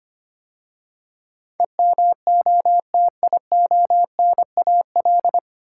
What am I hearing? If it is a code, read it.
Morse code, heard as a keyed tone, about 25 wpm, 700 Hz. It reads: EMOTIONAL
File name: practice / 25wpm / 9 Letter Words - First 3 Letters Given - 25wpm